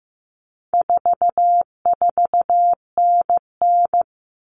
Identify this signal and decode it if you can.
Morse code, heard as a keyed tone, about 15 wpm, 700 Hz. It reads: 44NN
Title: Common Cut Numbers - 4-Digit 15wpm